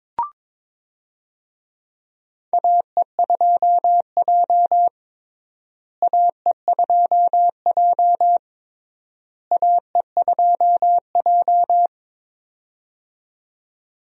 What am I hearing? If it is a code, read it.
Morse code, heard as a keyed tone, about 22 wpm, 700 Hz. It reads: AE2J AE2J AE2J
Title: USA Call Signs - 4 Characters - Repeated 3x - 22wpm